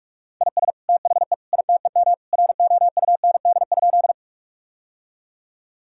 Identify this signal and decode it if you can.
Morse code, heard as a keyed tone, about 46 wpm, 700 Hz. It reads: IS THE ITEM ROUND?
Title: Sentences from Top 1000 Words - Review Part 6 - 45wpm